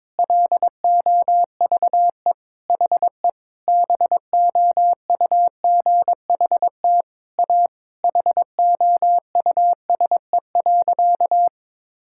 Morse code, heard as a keyed tone, about 22 wpm, 700 Hz. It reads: LOVE HE BOUGHT A HOUSE.
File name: practice / 22wpm / Sentences with 4-Word Phrases - 22wpm